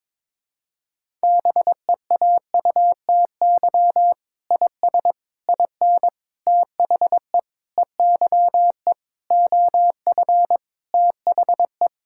Morse code, heard as a keyed tone, about 22 wpm, 700 Hz. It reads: BEAUTY IS IN THE EYE OF THE
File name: Sentences with 7-Word Phrases - 22wpm